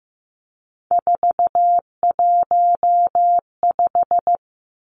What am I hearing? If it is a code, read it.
Morse code, heard as a keyed tone, about 15 wpm, 700 Hz. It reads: 415
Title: Numbers - 3-Digit 15wpm